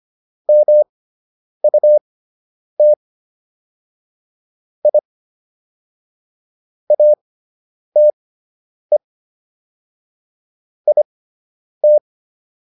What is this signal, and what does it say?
Morse code, heard as a keyed tone, about 25 wpm, spaced as for 7 wpm, 600 Hz. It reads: MUTIATE IT